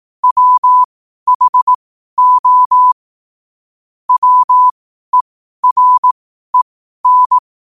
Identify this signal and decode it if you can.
Morse code, heard as a keyed tone, about 18 wpm, spaced as for 12 wpm, 1000 Hz. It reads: WHO WEREN
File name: qotdfiles / 12wpm